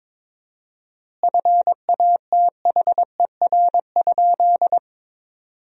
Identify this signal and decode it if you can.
Morse code, heard as a keyed tone, about 22 wpm, 700 Hz. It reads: FATHER?
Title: 4x Word-Spacing - Sentences from Top 300 Words - 22wpm